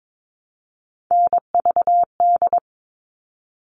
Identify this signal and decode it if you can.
Morse code, heard as a keyed tone, about 22 wpm, 700 Hz. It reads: NVD
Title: All Cut Numbers - 3-Digit 22wpm